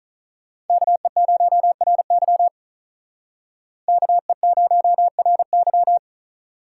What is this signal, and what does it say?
Morse code, heard as a keyed tone, about 46 wpm, 700 Hz. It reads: KE0RY KE0RY